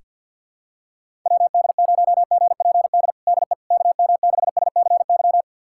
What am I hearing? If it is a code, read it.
Morse code, heard as a keyed tone, about 52 wpm, 700 Hz. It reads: WD0GPD DE KM6SCY
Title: USA Callsign DE Callsign 50wpm